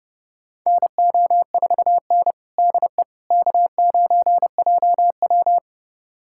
Morse code, heard as a keyed tone, about 30 wpm, 700 Hz. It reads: NO4N DE K9JW
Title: USA Callsign DE Callsign - 4 Characters - 30wpm